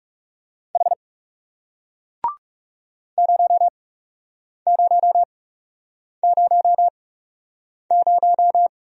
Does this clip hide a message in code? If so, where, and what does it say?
Yes: Morse code from 0.8 to 8.7 s, heard as a keyed tone, about 40 wpm, spaced as for 8 wpm, 700 Hz. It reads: H 0000